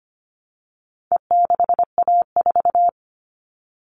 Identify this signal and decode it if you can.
Morse code, heard as a keyed tone, about 25 wpm, 700 Hz. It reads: E6A4
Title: All Cut Numbers - 4-Digit 25wpm